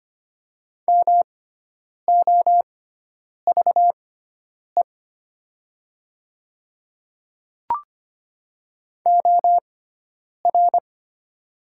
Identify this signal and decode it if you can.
Morse code, heard as a keyed tone, about 25 wpm, spaced as for 9 wpm, 690 Hz. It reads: MOVE OR
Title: 6x Character Spacing - Top 200 Words - 25wpm